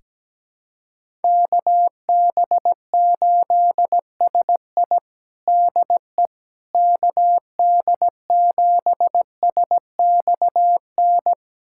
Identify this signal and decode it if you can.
Morse code, heard as a keyed tone, about 17 wpm, 700 Hz. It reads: KB8SI DE KD7SXN
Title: USA Callsign DE Callsign 17wpm